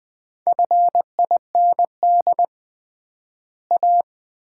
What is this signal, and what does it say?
Morse code, heard as a keyed tone, about 20 wpm, 700 Hz. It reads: FIND A
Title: 3x Word-Spacing - Sets of 4 Words - 20wpm